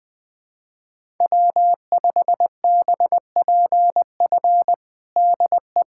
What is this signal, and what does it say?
Morse code, heard as a keyed tone, about 21 wpm, 700 Hz. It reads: W5BPF DE
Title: USA Callsign DE Callsign 20wpm